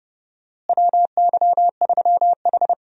Morse code, heard as a keyed tone, about 30 wpm, 700 Hz. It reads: WY3H
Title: USA Call Signs - 4 Characters - Repeated 2x - 30wpm